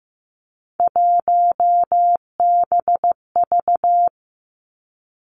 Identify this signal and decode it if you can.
Morse code, heard as a keyed tone, about 15 wpm, 700 Hz. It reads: 1BV
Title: Three Challenging Letter-Number Combinations 15wpm